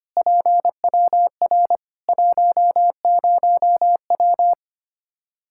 Morse code, heard as a keyed tone, about 25 wpm, 700 Hz. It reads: PWR 10W
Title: Practice QSO - Exchange 2 - Info - 25wpm